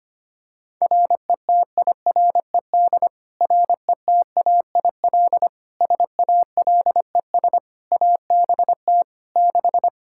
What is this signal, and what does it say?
Morse code, heard as a keyed tone, about 25 wpm, 700 Hz. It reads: RETIRED RETAIL SALES ABT 6
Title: Practice QSO - Exchange 3 - More Info - 25wpm